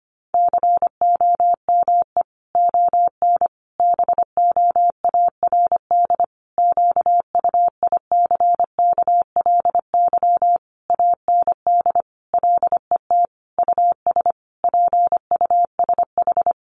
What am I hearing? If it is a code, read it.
Morse code, heard as a keyed tone, about 25 wpm, 700 Hz. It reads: COME ON BOARD QUICKLY AND LET US PUSH
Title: Sentences from Top 1300 Words 25wpm